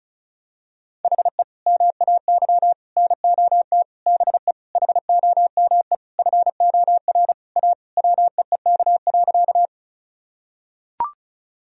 Morse code, heard as a keyed tone, about 36 wpm, 700 Hz. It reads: HE MAY NOT BE HOME FOR A WEEK.
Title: Sentences from Top 1000 Words - Review Part 2 - 35wpm